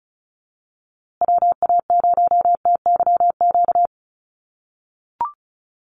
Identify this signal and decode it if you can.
Morse code, heard as a keyed tone, about 34 wpm, 700 Hz. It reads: WA0TYQ
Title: USA Call Signs Repeated 2x 35wpm